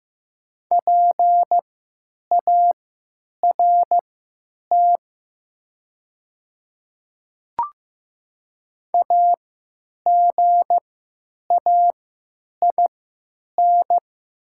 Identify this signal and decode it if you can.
Morse code, heard as a keyed tone, about 15 wpm, spaced as for 9 wpm, 700 Hz. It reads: PART AGAIN
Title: 3x Character Spacing - Top 300 Words - 15wpm